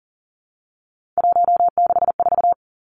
Morse code, heard as a keyed tone, about 40 wpm, 700 Hz. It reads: 164
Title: Numbers - 3-Digit 40wpm